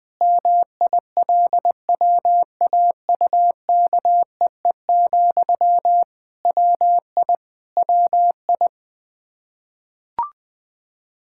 Morse code, heard as a keyed tone, about 21 wpm, 700 Hz. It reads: MILWAUKEE, WI WI
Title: K1USN SST QSO Practice 20wpm